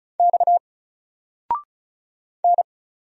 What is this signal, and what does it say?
Morse code, heard as a keyed tone, about 35 wpm, 700 Hz. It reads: X N
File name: Single-Letter-Number-35wpm